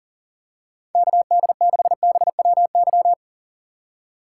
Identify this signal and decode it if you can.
Morse code, heard as a keyed tone, about 40 wpm, 700 Hz. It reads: KD6BWY
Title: USA Call Signs - 6 Characters - 4x Character Spacing - 40wpm